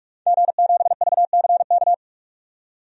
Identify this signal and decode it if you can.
Morse code, heard as a keyed tone, about 46 wpm, 700 Hz. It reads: G7VCK
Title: DXCC-100 Call Signs - 5 Characters - 45wpm